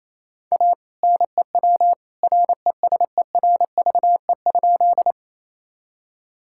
Morse code, heard as a keyed tone, about 28 wpm, 700 Hz. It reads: A NEW RESERVE?
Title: Sentences from Top 1000 Words - Review Part 6 - 28wpm